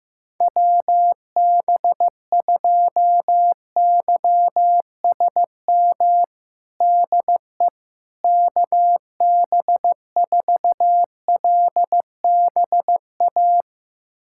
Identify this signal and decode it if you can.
Morse code, heard as a keyed tone, about 16 wpm, 700 Hz. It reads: WB2YSM DE KB4LBA
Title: USA Callsign DE Callsign 15wpm